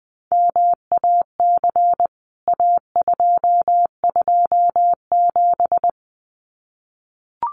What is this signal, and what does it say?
Morse code, heard as a keyed tone, about 20 wpm, 700 Hz. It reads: MAC A227
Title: CWT Exchange 20wpm